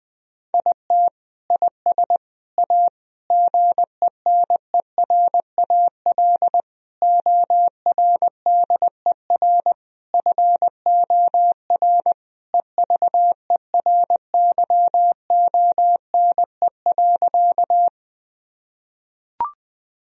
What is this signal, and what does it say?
Morse code, heard as a keyed tone, about 20 wpm, 700 Hz. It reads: IT IS A GENERAL ORDER FOR EVERYONE.
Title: Sentences from Top 1000 Words - Review Part 3 - 20wpm